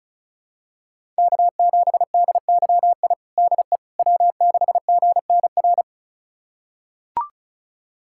Morse code, heard as a keyed tone, about 34 wpm, 700 Hz. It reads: K7DYI DE W6GNR
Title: USA Callsign DE Callsign - 5 Characters - 35wpm